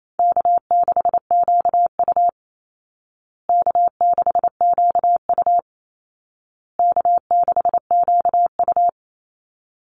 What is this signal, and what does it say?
Morse code, heard as a keyed tone, about 28 wpm, 700 Hz. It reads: K6QU K6QU K6QU
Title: Easy Call Signs Repeated 3x - 28wpm